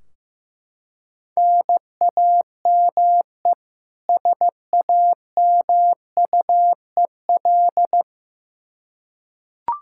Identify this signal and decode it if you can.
Morse code, heard as a keyed tone, about 15 wpm, 700 Hz. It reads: NAME SAMUEL
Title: Practice QSO - Exchange 1 - Beginning - 15wpm